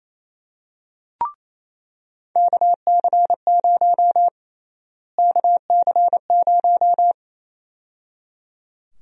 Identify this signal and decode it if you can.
Morse code, heard as a keyed tone, about 28 wpm, 700 Hz. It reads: KC0 KC0
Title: USA Call Sign Prefixes - Repeated 2x - 28wpm